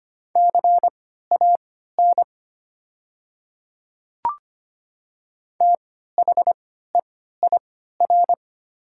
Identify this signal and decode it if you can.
Morse code, heard as a keyed tone, about 25 wpm, spaced as for 14 wpm, 700 Hz. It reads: CAN THEIR
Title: 3x Character Spacing - Top 100 Words - 25wpm